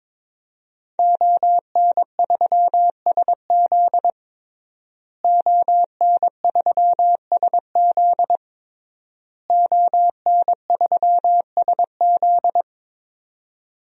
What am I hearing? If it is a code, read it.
Morse code, heard as a keyed tone, about 22 wpm, 700 Hz. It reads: ON3SZ ON3SZ ON3SZ
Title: DXCC-100 Call Signs - 5 Characters - Repeated 3x - 22wpm